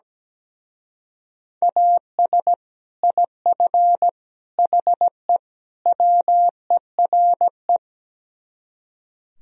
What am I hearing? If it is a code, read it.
Morse code, heard as a keyed tone, about 17 wpm, 700 Hz. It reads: AS IF HE WERE
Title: Speed-Racing - Sets of 4 Words 17wpm